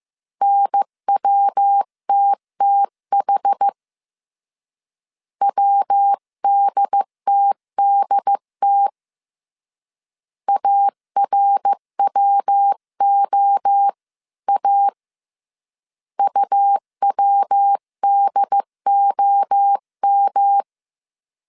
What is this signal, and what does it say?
Morse code, heard as a keyed tone, about 15 wpm, 800 Hz. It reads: NWTTH WDTDT ARWOA UWDOM
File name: CW / DET / 1 3 1500C S A 14wpm 2447